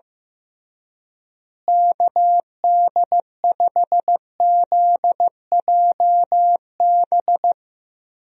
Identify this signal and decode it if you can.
Morse code, heard as a keyed tone, about 15 wpm, 700 Hz. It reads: KD5ZJB
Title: USA Call Signs - 6 Characters - Repeated 3x - 15wpm